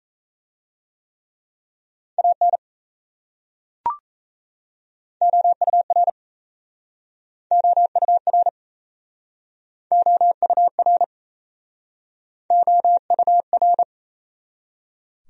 Morse code, heard as a keyed tone, about 37 wpm, 700 Hz. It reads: AN OUR OUR OUR OUR